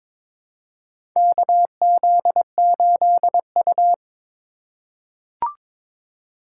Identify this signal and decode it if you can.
Morse code, heard as a keyed tone, about 22 wpm, 700 Hz. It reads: KZ8U
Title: USA Call Signs - 4 Characters - 3x Character Spacing - 22wpm